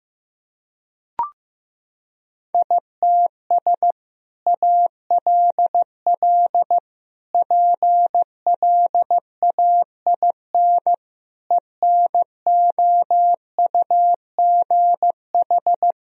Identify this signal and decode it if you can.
Morse code, heard as a keyed tone, about 16 wpm, 700 Hz. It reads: ITS ALL PLAIN ENOUGH
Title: Sentences from Top 1000 Words - Review Part 2 - 15wpm